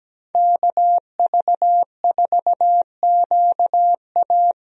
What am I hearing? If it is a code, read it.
Morse code, heard as a keyed tone, about 17 wpm, 700 Hz. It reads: KV4QA
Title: USA Call Signs - 5 Characters - Repeated 3x - 17wpm